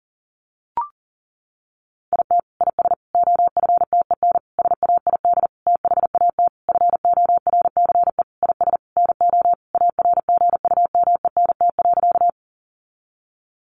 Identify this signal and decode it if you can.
Morse code, heard as a keyed tone, about 40 wpm, 700 Hz. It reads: IT IS OFTEN SAID THAT FORCE IS NO ARGUMENT.
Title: Sentences from Top 1400 Words 40wpm